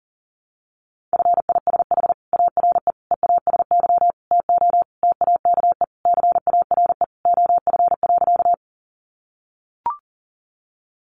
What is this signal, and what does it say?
Morse code, heard as a keyed tone, about 40 wpm, 700 Hz. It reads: FISH ARE EASY TO TAKE CARE OF.
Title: Additional Easy Phrases 40wpm